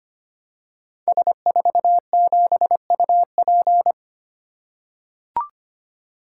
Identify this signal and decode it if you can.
Morse code, heard as a keyed tone, about 25 wpm, 700 Hz. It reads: S47UP
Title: Code Groups - 5 Letter:Numbers - 25wpm